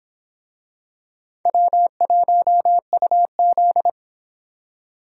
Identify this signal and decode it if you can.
Morse code, heard as a keyed tone, about 24 wpm, 700 Hz. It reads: W1UZ